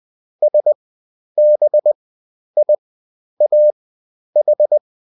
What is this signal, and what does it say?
Morse code, heard as a keyed tone, about 20 wpm, spaced as for 10 wpm, 600 Hz. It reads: SBIAH